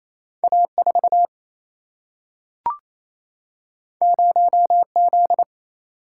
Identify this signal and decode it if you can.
Morse code, heard as a keyed tone, about 28 wpm, 700 Hz. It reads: A 4 0 Z
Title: Single-Letter Single-Number 28wpm